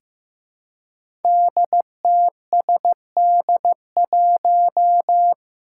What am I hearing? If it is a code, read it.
Morse code, heard as a keyed tone, about 15 wpm, 700 Hz. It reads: DTSD1
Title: Five Challenging Letter-Number Combinations 15wpm